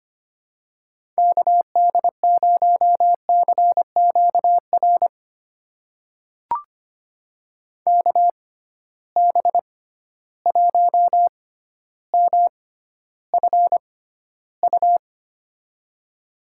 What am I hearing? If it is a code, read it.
Morse code, heard as a keyed tone, about 25 wpm, spaced as for 13 wpm, 700 Hz. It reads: KD0CQR KB1MFU
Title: USA Call Signs - 6 Characters - 6x Character Spacing - 25wpm